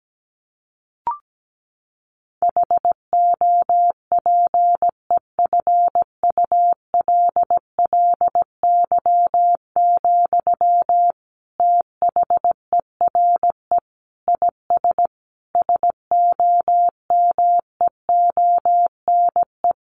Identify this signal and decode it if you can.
Morse code, heard as a keyed tone, about 17 wpm, 700 Hz. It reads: HOPEFULLY, THERE IS SOMEONE
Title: Sentences from Top 1500 Words 17wpm